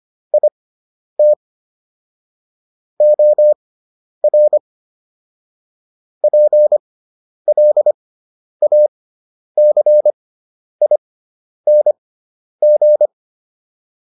Morse code, heard as a keyed tone, about 25 wpm, spaced as for 10 wpm, 600 Hz. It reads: IT OR PLACING